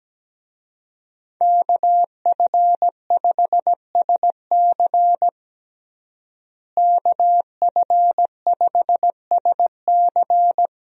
Morse code, heard as a keyed tone, about 17 wpm, 700 Hz. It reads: KF5SC KF5SC
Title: USA Call Signs - 5 Characters - Repeated 2x - 17wpm